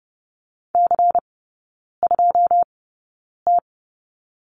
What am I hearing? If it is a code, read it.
Morse code, heard as a keyed tone, about 30 wpm, spaced as for 9 wpm, 700 Hz. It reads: C2T
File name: Lesson 17e - Letter u - Three Characters 30wpm